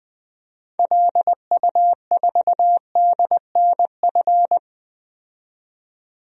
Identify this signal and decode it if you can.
Morse code, heard as a keyed tone, about 20 wpm, 700 Hz. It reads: LU4DNF
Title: DXCC-100 Call Signs - 6 Characters - 20wpm